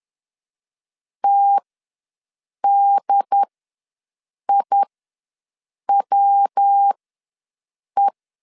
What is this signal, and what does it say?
Morse code, heard as a keyed tone, about 10 wpm, spaced as for 6 wpm, 800 Hz. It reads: TDIWE
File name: LEZIONI / 7 GF9Z / 7 GF9 ws 10wpm 4034